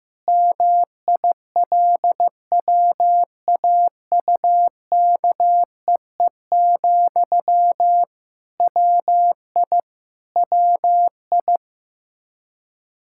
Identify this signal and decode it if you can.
Morse code, heard as a keyed tone, about 15 wpm, 700 Hz. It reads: MILWAUKEE, WI WI
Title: Practice QSO - Exchange 1 - Beginning - 15wpm